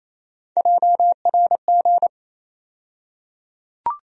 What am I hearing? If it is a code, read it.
Morse code, heard as a keyed tone, about 28 wpm, 700 Hz. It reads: JRG
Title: USA Call Sign Postfixes - 3 Letters 28wpm